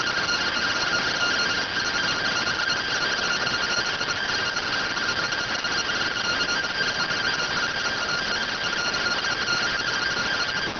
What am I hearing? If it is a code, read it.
Two-tone teleprinter signal, about 170 Hz shift.